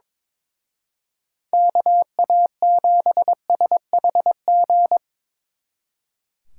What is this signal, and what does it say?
Morse code, heard as a keyed tone, about 24 wpm, 700 Hz. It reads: KA7SHG
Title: Speed Racing - USA 2x3 Call Signs 22wpm